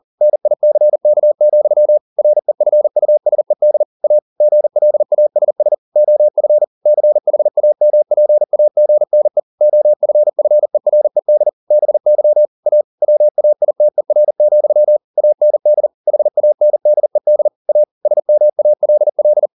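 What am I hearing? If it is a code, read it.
Morse code, heard as a keyed tone, about 40 wpm, 600 Hz. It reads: NICK, REFUSED A GLASS OF CHAMPAGNE OFFERED BY A WAITER, AND HANDED A SMALL
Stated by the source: Morse code at 40 wpm, practice text, CW med Gnister.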